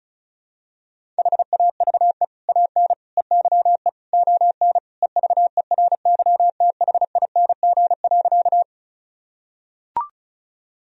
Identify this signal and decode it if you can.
Morse code, heard as a keyed tone, about 34 wpm, 700 Hz. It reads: HAVE AN EYE ON EVERYTHING.
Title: Sentences from Top 1000 Words - Review Part 1 - 35wpm